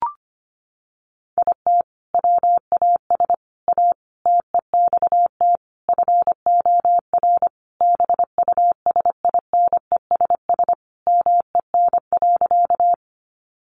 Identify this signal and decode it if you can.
Morse code, heard as a keyed tone, about 25 wpm, 700 Hz. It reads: IT WAS A TEXT FOR BUSINESS MEN.
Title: Sentences from Top 1200 Words 25wpm